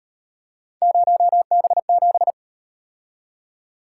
Morse code, heard as a keyed tone, about 38 wpm, 700 Hz. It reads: 0B7